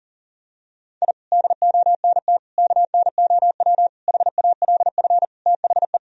Morse code, heard as a keyed tone, about 40 wpm, 700 Hz. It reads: I DONT KNOW HALF THE
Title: Sentences from Top 1000 Words - Review Part 1 - 40wpm